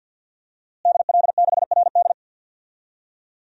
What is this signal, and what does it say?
Morse code, heard as a keyed tone, about 52 wpm, 700 Hz. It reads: DL6RD
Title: DXCC-100 Call Signs - 5 Characters - Repeated 2x - 50wpm